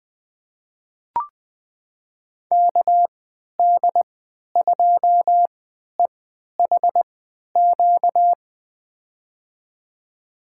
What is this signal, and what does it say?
Morse code, heard as a keyed tone, about 20 wpm, spaced as for 11 wpm, 700 Hz. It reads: KD2EHQ
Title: USA Call Signs - 6 Characters - 3x Character Spacing - 20wpm